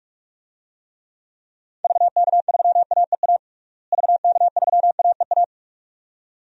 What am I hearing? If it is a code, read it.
Morse code, heard as a keyed tone, about 46 wpm, 700 Hz. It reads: VK3AEA VK3AEA
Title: DXCC-100 Call Signs - 6 Characters - Repeated 2x - 45wpm